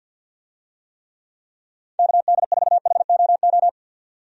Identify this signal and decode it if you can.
Morse code, heard as a keyed tone, about 47 wpm, 700 Hz. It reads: KD4HOO
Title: USA Call Signs - 6 Characters - 50wpm